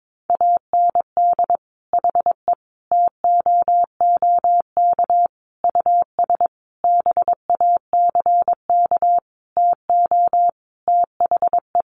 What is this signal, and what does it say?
Morse code, heard as a keyed tone, about 22 wpm, 700 Hz. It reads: AND HE TOOK US BACK TO THE
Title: Sentences from Top 1500 Words 22wpm